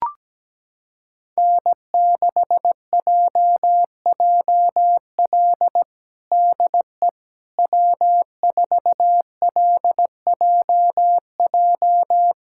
Morse code, heard as a keyed tone, about 17 wpm, 700 Hz. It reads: N6JJL DE W4LJJ
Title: USA Callsign DE Callsign - 5 Characters - 17wpm